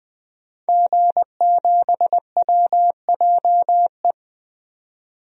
Morse code, heard as a keyed tone, about 20 wpm, 700 Hz. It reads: G7WJE
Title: DXCC-100 Call Signs - 5 Characters - 20wpm